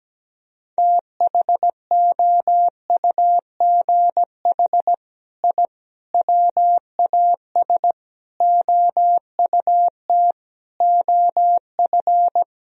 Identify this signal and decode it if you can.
Morse code, heard as a keyed tone, about 17 wpm, 700 Hz. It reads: THOUGH I WAS OUT OF
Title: Sentences from Top 1000 Words - Review Part 4 - 17wpm